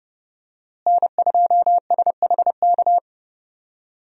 Morse code, heard as a keyed tone, about 28 wpm, 700 Hz. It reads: N2SHK
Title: USA Call Signs - 5 Characters - Repeated 3x - 30wpm